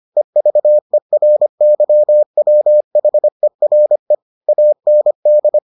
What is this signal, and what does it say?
Morse code, heard as a keyed tone, about 25 wpm, 600 Hz. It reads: EVERYWHERE AND